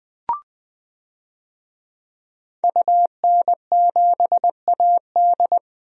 Morse code, heard as a keyed tone, about 20 wpm, 700 Hz. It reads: UN7AD
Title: DXCC-100 Call Signs - 5 Characters - Repeated 3x - 20wpm